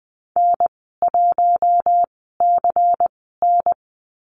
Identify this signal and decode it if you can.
Morse code, heard as a keyed tone, about 19 wpm, spaced as for 14 wpm, 700 Hz. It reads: N1CN
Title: 173 Lesson 28g - Number 9 - Call Signs 20wpm